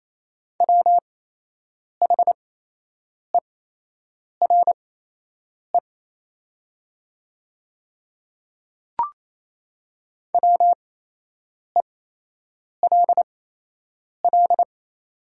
Morse code, heard as a keyed tone, about 28 wpm, spaced as for 8 wpm, 700 Hz. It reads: WHERE WELL
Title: 8x Character Spacing - Top 200 Words - 28wpm